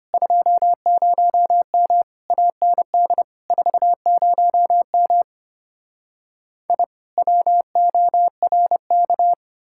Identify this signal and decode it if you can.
Morse code, heard as a keyed tone, about 33 wpm, 700 Hz. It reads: 20M AND 40M I WORK